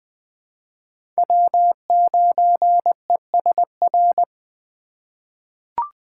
Morse code, heard as a keyed tone, about 20 wpm, 700 Hz. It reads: W9ESR
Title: USA Call Signs - 5 Characters - 3x Character Spacing - 20wpm